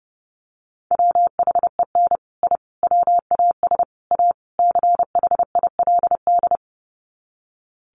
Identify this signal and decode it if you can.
Morse code, heard as a keyed tone, about 30 wpm, 700 Hz. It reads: WHEN I WAS A CHILD
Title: Sets of 5 Words Encore 30wpm